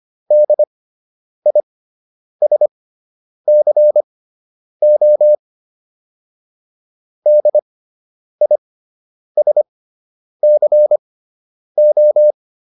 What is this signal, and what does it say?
Morse code, heard as a keyed tone, about 25 wpm, spaced as for 9 wpm, 600 Hz. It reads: DISCO DISCO